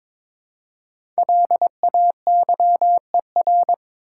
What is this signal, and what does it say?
Morse code, heard as a keyed tone, about 22 wpm, 690 Hz. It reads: LAYER
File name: 5 Letter Words - First 3 Letters Given - 22wpm